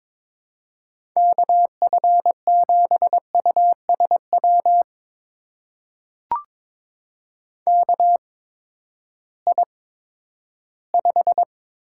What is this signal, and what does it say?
Morse code, heard as a keyed tone, about 22 wpm, spaced as for 12 wpm, 700 Hz. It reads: KF7USW KI5
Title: USA Call Signs - 6 Characters - 8x Character Spacing - 22wpm